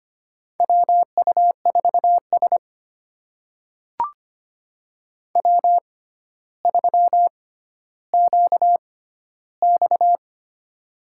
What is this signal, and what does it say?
Morse code, heard as a keyed tone, about 25 wpm, spaced as for 13 wpm, 700 Hz. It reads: WU4S W3QX
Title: USA Call Signs - 6x Character Spacing - 25wpm